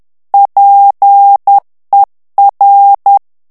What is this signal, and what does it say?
Morse code, heard as a keyed tone, about 10 wpm, 800 Hz. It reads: PER
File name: morse mp3 / ITALIEN texte clair 1 vitesse 10 mots minute